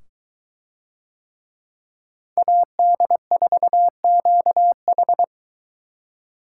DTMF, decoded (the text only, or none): none